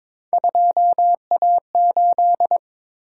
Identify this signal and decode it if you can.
Morse code, heard as a keyed tone, about 22 wpm, 700 Hz. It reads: 2A8
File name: practice / 22wpm / Three Letter-Number Combinations - 22wpm